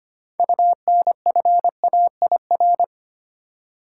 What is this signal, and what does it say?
Morse code, heard as a keyed tone, about 25 wpm, 700 Hz. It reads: UNFAIR
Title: Six Letter Words - 25wpm